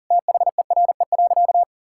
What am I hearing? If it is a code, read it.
Morse code, heard as a keyed tone, about 37 wpm, 700 Hz. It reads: THERE.
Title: Sentences from Top 1000 Words - Review Part 5 - 40wpm